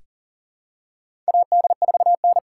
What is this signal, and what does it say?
Morse code, heard as a keyed tone, about 40 wpm, 700 Hz. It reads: AD4N
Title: USA Call Signs - 4 Characters - 4x Character Spacing - 40wpm